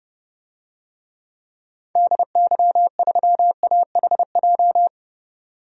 Morse code, heard as a keyed tone, about 31 wpm, 700 Hz. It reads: DY3AHJ
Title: DXCC-100 Call Signs - 6 Characters - 30wpm